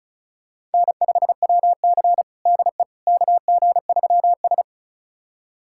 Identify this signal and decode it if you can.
Morse code, heard as a keyed tone, about 36 wpm, 700 Hz. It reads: N5WC DE KG3S
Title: USA Callsign DE Callsign - 4 Characters - 35wpm